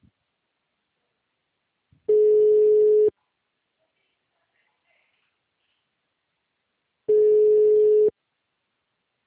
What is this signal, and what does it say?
Special information tone.
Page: Mogi Guacu - SP